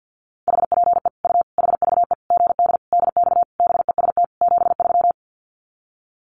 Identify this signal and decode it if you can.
Morse code, heard as a keyed tone, about 50 wpm, 700 Hz. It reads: HPE U HVE GD DX BEST 73
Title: Common QSO Phrases 50wpm